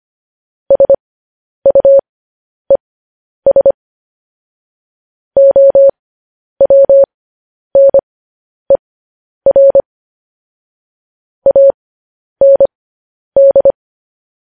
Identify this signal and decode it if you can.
Morse code, heard as a keyed tone, about 25 wpm, spaced as for 10 wpm, 550 Hz. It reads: SUES OWNER AND